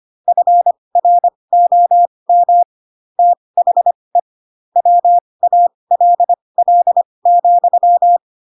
Morse code, heard as a keyed tone, about 25 wpm, spaced as for 20 wpm, 700 Hz. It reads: FROM THE WALL,
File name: Pinocchio / Pinocchio ch4 15 25 ik1hge